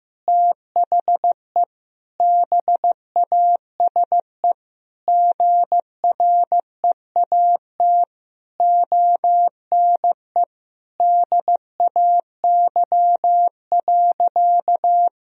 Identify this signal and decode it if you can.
Morse code, heard as a keyed tone, about 16 wpm, 700 Hz. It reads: THE BASE GREAT ONE DAY.
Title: Sentences from Top 1000 Words - Review Part 1 - 15wpm